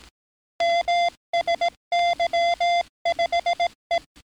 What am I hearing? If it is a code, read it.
Morse code, heard as a keyed tone, about 18 wpm, 690 Hz. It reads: MSY5E
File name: training / NAVPERS 11049to11060 / NAVPERS 11057 RA RB